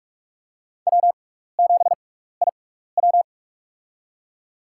Morse code, heard as a keyed tone, about 46 wpm, spaced as for 16 wpm, 700 Hz. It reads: W7IW
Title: USA Call Signs - 4 Characters - 6x Character Spacing - 45wpm